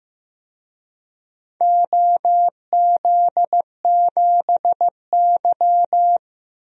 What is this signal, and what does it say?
Morse code, heard as a keyed tone, about 16 wpm, 700 Hz. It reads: OZ7Y
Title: DXCC-100 Call Signs - 4 Characters - 15wpm